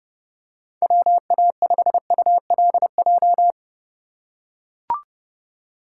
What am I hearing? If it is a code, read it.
Morse code, heard as a keyed tone, about 29 wpm, 700 Hz. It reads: WA5ULJ